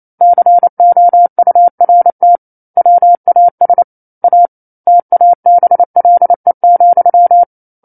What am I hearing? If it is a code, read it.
Morse code, heard as a keyed tone, about 28 wpm, 700 Hz. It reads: COURT WAS A TABLE,
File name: 09-Chapter-09 28 WPM